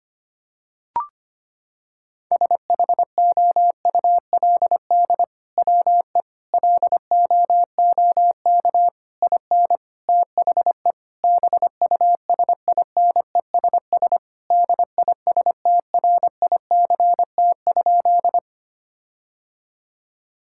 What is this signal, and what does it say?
Morse code, heard as a keyed tone, about 25 wpm, 700 Hz. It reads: SHOULD WE LOOK IN THE BUSINESS DISTRICT?
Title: Sentences from Top 1600 Words 25wpm